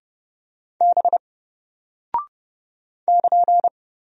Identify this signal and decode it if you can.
Morse code, heard as a keyed tone, about 30 wpm, 700 Hz. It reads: B <KN>
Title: Single Character (Letter:Number:Punctuation:Prosign) 30wpm